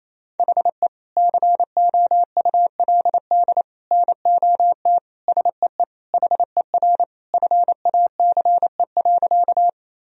Morse code, heard as a keyed tone, about 28 wpm, 700 Hz. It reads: HE COULD NOT SEE HER FACE.
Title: Sentences from Top 1000 Words - Review Part 2 - 28wpm